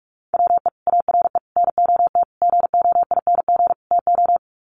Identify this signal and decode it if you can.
Morse code, heard as a keyed tone, about 44 wpm, 700 Hz. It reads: WE ARE NOT GOING TO
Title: Sets of 5 Words 45wpm